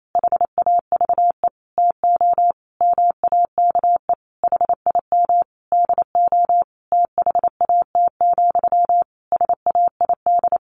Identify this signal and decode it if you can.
Morse code, heard as a keyed tone, about 28 wpm, 700 Hz. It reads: HAVE TO MAKE HIM DO THAT, SAID
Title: A Sound Of Thunder 28wpm